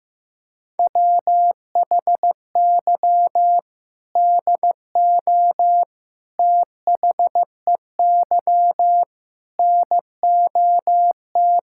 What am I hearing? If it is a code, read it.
Morse code, heard as a keyed tone, about 15 wpm, 700 Hz. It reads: WHY DO THEY NOT
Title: Sentences from Top 1600 Words 15wpm